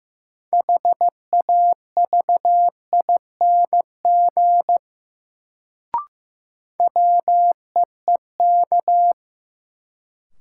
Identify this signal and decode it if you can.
Morse code, heard as a keyed tone, about 15 wpm, 700 Hz. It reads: HAVING WEEK
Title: Sets of 5 Words Encore - New Word Drill - 15wpm